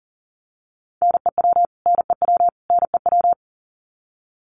Morse code, heard as a keyed tone, about 40 wpm, 700 Hz. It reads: NEW NEW NEW
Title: Top 200 Words Repeated 40wpm